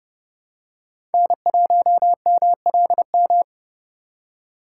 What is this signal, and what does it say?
Morse code, heard as a keyed tone, about 31 wpm, 700 Hz. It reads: N1MLM
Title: USA Call Signs - 4x Character Spacing - 30wpm